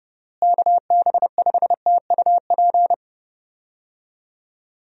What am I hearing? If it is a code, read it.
Morse code, heard as a keyed tone, about 30 wpm, 700 Hz. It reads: KB5TUP
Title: USA Call Signs - 6 Characters - Repeated 2x - 30wpm